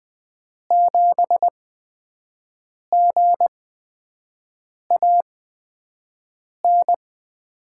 Morse code, heard as a keyed tone, about 20 wpm, spaced as for 5 wpm, 700 Hz. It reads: 7GAN
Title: USA Call Signs - 5 Characters - 8x Character Spacing - 20wpm